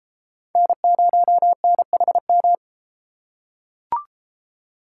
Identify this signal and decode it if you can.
Morse code, heard as a keyed tone, about 29 wpm, 700 Hz. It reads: N0NHM